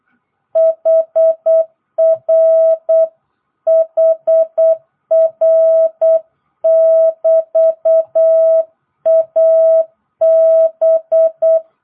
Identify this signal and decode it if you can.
Morse code, heard as a keyed tone, about 8 wpm, 650 Hz. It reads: HR HR <BT> AB